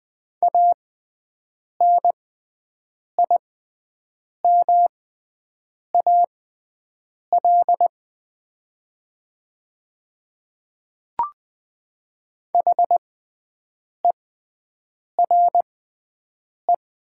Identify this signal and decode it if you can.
Morse code, heard as a keyed tone, about 20 wpm, spaced as for 7 wpm, 700 Hz. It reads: ANIMAL HERE
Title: 6x Character Spacing - Top 200 Words - 20wpm